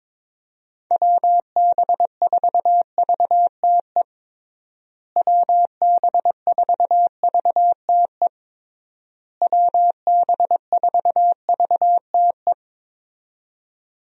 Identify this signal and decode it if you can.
Morse code, heard as a keyed tone, about 22 wpm, 700 Hz. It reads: WB4VTE WB4VTE WB4VTE
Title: USA Call Signs - 6 Characters - Repeated 3x - 22wpm